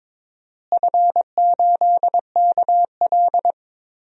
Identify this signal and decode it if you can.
Morse code, heard as a keyed tone, about 22 wpm, 700 Hz. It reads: F8KL
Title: DXCC-100 Call Signs - 4 Characters - Repeated 3x - 22wpm